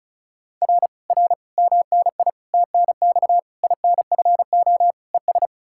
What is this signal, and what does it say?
Morse code, heard as a keyed tone, about 46 wpm, 700 Hz. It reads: R R MNI TNX INFO ES